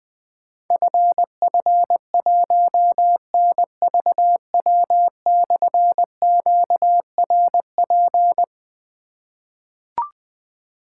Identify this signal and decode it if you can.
Morse code, heard as a keyed tone, about 20 wpm, 700 Hz. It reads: FF1NVW/QRP
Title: Difficult Call Signs 20wpm